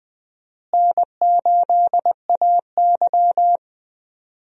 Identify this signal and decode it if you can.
Morse code, heard as a keyed tone, about 20 wpm, 700 Hz. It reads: N8AY
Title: USA Call Signs - 4 Characters - Repeated 3x - 20wpm